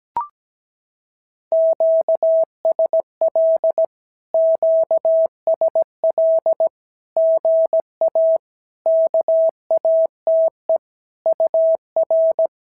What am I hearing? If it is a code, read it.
Morse code, heard as a keyed tone, about 17 wpm, 650 Hz. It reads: QSL QSL GA KATE UR
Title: SOTA QSO Practice 17wpm